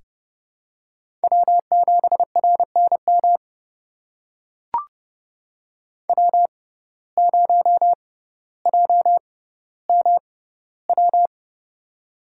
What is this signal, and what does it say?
Morse code, heard as a keyed tone, about 30 wpm, spaced as for 15 wpm, 700 Hz. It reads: W7RNM W0JMW
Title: USA Call Signs - 5 Characters - 6x Character Spacing - 30wpm